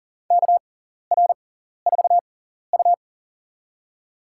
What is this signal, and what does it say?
Morse code, heard as a keyed tone, about 37 wpm, spaced as for 14 wpm, 700 Hz. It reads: KR4U
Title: USA Call Signs - 4 Characters - 6x Character Spacing - 40wpm